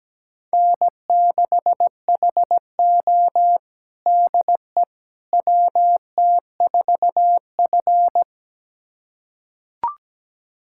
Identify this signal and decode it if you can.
Morse code, heard as a keyed tone, about 17 wpm, 700 Hz. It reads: N6HO DE WT4F
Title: USA Callsign DE Callsign - 4 Characters - 17wpm